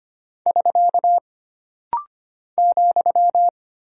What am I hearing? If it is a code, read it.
Morse code, heard as a keyed tone, about 25 wpm, 700 Hz. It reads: <SK> ,
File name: Single Punctuation-Prosign - Rapid-Fire - 25wpm